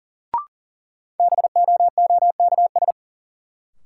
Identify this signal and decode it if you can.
Morse code, heard as a keyed tone, about 40 wpm, 700 Hz. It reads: BOOKS
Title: Sentences from Top 1000 Words - Review Part 4 - New Word Drill - 40wpm